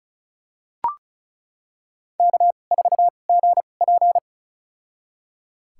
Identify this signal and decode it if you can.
Morse code, heard as a keyed tone, about 36 wpm, spaced as for 25 wpm, 700 Hz. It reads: K4GP
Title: USA Call Signs - 4 Characters - 2x Character Spacing - 35wpm